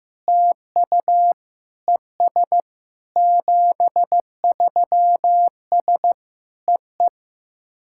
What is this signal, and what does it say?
Morse code, heard as a keyed tone, about 15 wpm, 700 Hz. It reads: TU ES 73S EE
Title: SOTA QSO Practice 15wpm